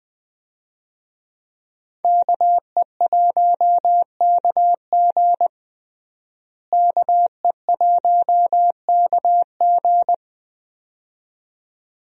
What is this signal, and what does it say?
Morse code, heard as a keyed tone, about 20 wpm, 700 Hz. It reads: KE1KG KE1KG
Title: USA Call Signs - 5 Characters - Repeated 2x - 20wpm